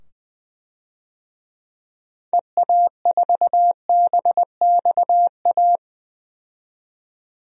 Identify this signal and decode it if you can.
Morse code, heard as a keyed tone, about 20 wpm, 700 Hz. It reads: EA4BXA
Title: DXCC-100 Call Signs - 6 Characters - 20wpm